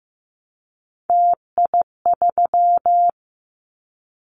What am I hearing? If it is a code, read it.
Morse code, heard as a keyed tone, about 15 wpm, 700 Hz. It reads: TI3
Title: Callsign Prefixes for Top 100 DXCC - 15wpm